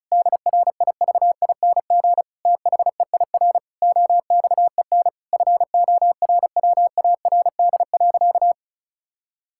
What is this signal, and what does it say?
Morse code, heard as a keyed tone, about 34 wpm, 700 Hz. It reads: DRIVING THEIR OXEN FORWARD.
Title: Pangram Challenge v2 - 35wpm